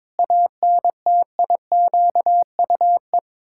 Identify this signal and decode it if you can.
Morse code, heard as a keyed tone, about 22 wpm, 700 Hz. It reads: ANTIQUE
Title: Pangram Challenge v2 - 22wpm